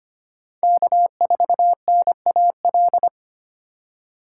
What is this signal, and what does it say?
Morse code, heard as a keyed tone, about 33 wpm, 700 Hz. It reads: K4NAL